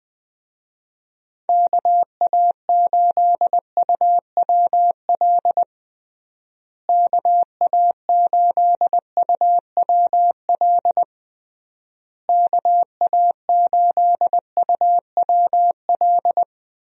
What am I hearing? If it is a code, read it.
Morse code, heard as a keyed tone, about 21 wpm, 700 Hz. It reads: KA8UWL KA8UWL KA8UWL
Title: USA Call Signs - 6 Characters - Repeated 3x - 20wpm